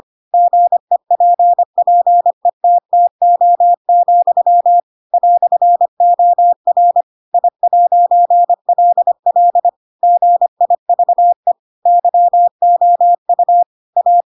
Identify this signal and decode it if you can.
Morse code, heard as a keyed tone, about 25 wpm, 700 Hz. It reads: GEPPETTO, "OR I'LL GIVE YOU A
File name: Pinocchio / Pinocchio ch7 25 25 ik1hge